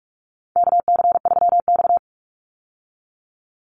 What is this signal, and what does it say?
Morse code, heard as a keyed tone, about 44 wpm, 700 Hz. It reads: KC3X
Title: Call Signs Repeated 3x - 45wpm